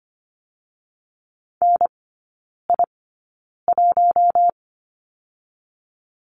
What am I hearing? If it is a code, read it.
Morse code, heard as a keyed tone, about 25 wpm, spaced as for 9 wpm, 690 Hz. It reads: NI1
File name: Lesson 14e - Number 2 - Three Characters 25wpm